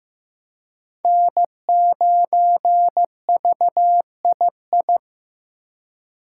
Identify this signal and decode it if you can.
Morse code, heard as a keyed tone, about 15 wpm, 700 Hz. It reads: N9VII
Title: USA Call Signs - 5 Characters - 2x Character Spacing - 15wpm